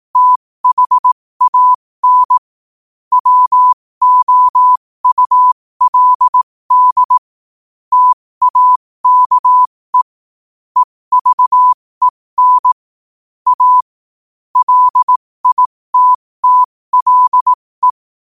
Morse code, heard as a keyed tone, about 18 wpm, spaced as for 15 wpm, 1000 Hz. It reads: THAN WOULD TAKE EVEN A LITTLE
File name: qotdfiles / 15wpm